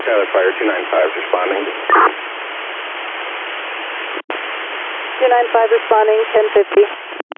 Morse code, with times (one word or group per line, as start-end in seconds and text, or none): none